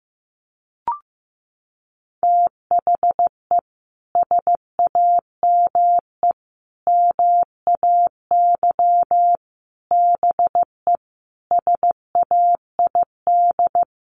Morse code, heard as a keyed tone, about 15 wpm, 700 Hz. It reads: THE SAME MAY BE SAID
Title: Sentences from Top 1100 Words 15wpm